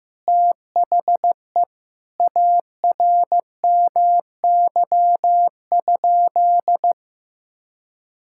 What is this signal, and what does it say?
Morse code, heard as a keyed tone, about 15 wpm, 700 Hz. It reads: THE ARMY?
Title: Sentences from Top 1000 Words - Review Part 5 - 15wpm